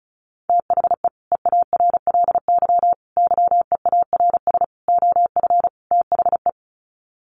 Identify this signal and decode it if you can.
Morse code, heard as a keyed tone, about 34 wpm, 700 Hz. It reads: THE EARLY YEARS OF THE
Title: Sets of 5 Words Encore 35wpm